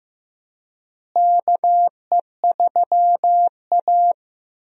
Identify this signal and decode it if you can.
Morse code, heard as a keyed tone, about 15 wpm, 700 Hz. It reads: KE3A
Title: USA Call Signs - 4 Characters - Repeated 2x - 15wpm